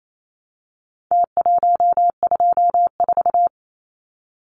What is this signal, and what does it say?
Morse code, heard as a keyed tone, about 28 wpm, 700 Hz. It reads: T124
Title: Numbers - Contest Style - 4-Digit 28wpm